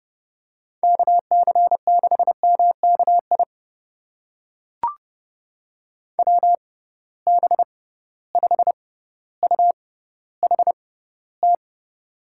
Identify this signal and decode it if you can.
Morse code, heard as a keyed tone, about 32 wpm, spaced as for 16 wpm, 700 Hz. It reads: KC6MKI WB5UHT
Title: USA Call Signs - 6 Characters - 6x Character Spacing - 30wpm